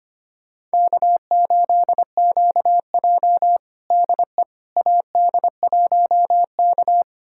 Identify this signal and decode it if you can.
Morse code, heard as a keyed tone, about 25 wpm, 700 Hz. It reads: K8QJ DE AD1K
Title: USA Callsign DE Callsign - 4 Characters - 25wpm